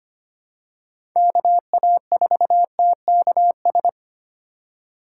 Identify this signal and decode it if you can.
Morse code, heard as a keyed tone, about 25 wpm, 700 Hz. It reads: KA4TKS
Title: USA Call Signs - 6 Characters - 3x Character Spacing - 25wpm